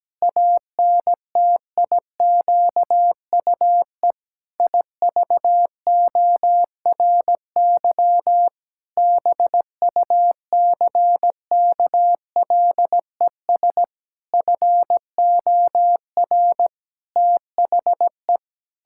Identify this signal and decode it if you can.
Morse code, heard as a keyed tone, about 17 wpm, 700 Hz. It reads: ANTIQUE IVORY BUCKLES FOR THE
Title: Pangram Challenge v2 - 17wpm